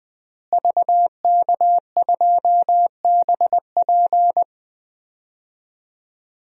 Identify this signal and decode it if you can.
Morse code, heard as a keyed tone, about 20 wpm, 700 Hz. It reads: VK2BP
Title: DXCC-100 Call Signs - 5 Characters - Repeated 2x - 20wpm